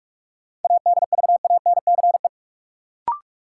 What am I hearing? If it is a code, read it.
Morse code, heard as a keyed tone, about 46 wpm, 700 Hz. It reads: ADVANCE
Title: Sentences from Top 1400 Words - New Word Drill Alt - 45wpm